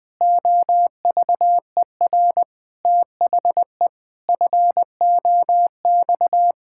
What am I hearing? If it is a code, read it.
Morse code, heard as a keyed tone, about 20 wpm, 700 Hz. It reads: OVER THE FOX
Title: Pangram Challenge v2 - 20wpm